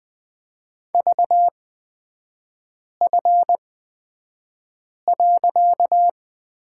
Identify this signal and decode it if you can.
Morse code, heard as a keyed tone, about 20 wpm, spaced as for 5 wpm, 700 Hz. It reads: VF.
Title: 255 Lesson 41e - Period - Three Characters 20wpm